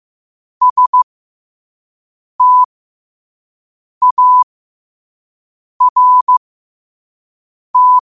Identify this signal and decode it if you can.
Morse code, heard as a keyed tone, about 15 wpm, spaced as for 5 wpm, 1000 Hz. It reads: START
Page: 5 WPM morse code quotes for Wed, 30 Jul 2025.